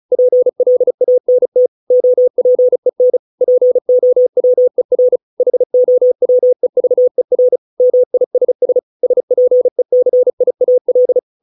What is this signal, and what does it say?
Morse code, heard as a keyed tone, about 35 wpm, 500 Hz. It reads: PLANT OPEN POWER HOWEVER MISS SPECIAL